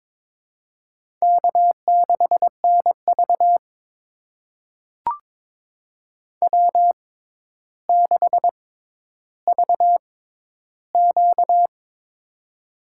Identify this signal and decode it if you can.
Morse code, heard as a keyed tone, about 22 wpm, spaced as for 11 wpm, 700 Hz. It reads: K6NV W6VQ
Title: USA Call Signs - 4 Characters - 6x Character Spacing - 22wpm